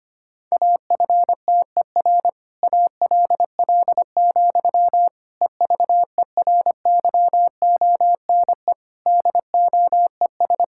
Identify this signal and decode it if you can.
Morse code, heard as a keyed tone, about 25 wpm, 700 Hz. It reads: AFTER ALL, EVERYONE DOES
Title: Sentences from Top 1000 Words - Review Part 5 - 25wpm